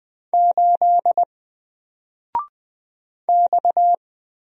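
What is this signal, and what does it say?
Morse code, heard as a keyed tone, about 20 wpm, 700 Hz. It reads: 8 X
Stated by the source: Single letter and number Morse drill at 20 wpm, from N4LCD.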